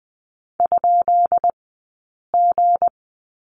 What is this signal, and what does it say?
Morse code, heard as a keyed tone, about 20 wpm, 700 Hz. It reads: ? G
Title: 217 Lesson 35d - Number 0 - Two Characters 20wpm